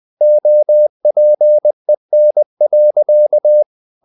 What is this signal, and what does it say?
Morse code, heard as a keyed tone, about 20 wpm, 600 Hz. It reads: OPEN.
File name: the old man and the sea part 2 20wpm